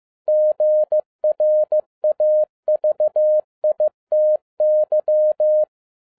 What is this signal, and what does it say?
Morse code, heard as a keyed tone, about 15 wpm, 610 Hz. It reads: GRAVITY